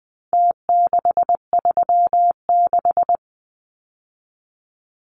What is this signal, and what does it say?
Morse code, heard as a keyed tone, about 20 wpm, 700 Hz. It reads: T636
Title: Numbers - Contest Style - 4-Digit 20wpm